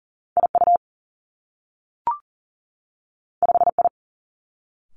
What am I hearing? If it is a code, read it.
Morse code, heard as a keyed tone, about 38 wpm, 700 Hz. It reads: I U 5 I
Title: Two Challenging Letter-Number Combinations 40wpm